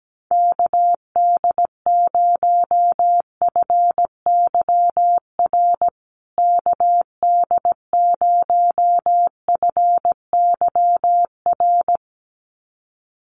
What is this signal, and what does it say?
Morse code, heard as a keyed tone, about 17 wpm, 700 Hz. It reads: KD0FYR KD0FYR
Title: USA Call Signs Repeated 2x 17wpm